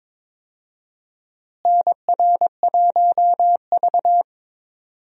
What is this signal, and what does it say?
Morse code, heard as a keyed tone, about 22 wpm, 700 Hz. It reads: NR1V
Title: USA Call Signs - 4 Characters - Repeated 2x - 22wpm